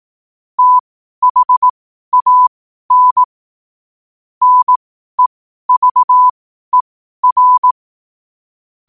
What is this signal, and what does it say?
Morse code, heard as a keyed tone, about 18 wpm, spaced as for 12 wpm, 1000 Hz. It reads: THAN NEVER
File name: qotdfiles / 12wpm